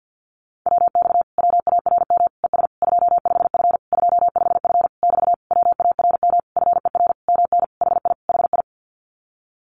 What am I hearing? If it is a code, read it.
Morse code, heard as a keyed tone, about 48 wpm, 700 Hz. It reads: WX WARM ES 25F 25F <BT> WARM FER MN HI HI
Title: Common QSO Phrases Encore 50wpm